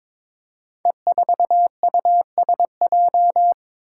Morse code, heard as a keyed tone, about 22 wpm, 700 Hz. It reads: E4USJ
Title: Five Challenging Letter-Number Combinations 22wpm